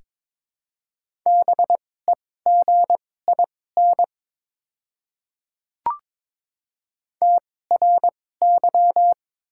Morse code, heard as a keyed tone, about 22 wpm, spaced as for 16 wpm, 700 Hz. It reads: BEGIN TRY
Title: 2x Character Spacing - Top 500 Words - 22wpm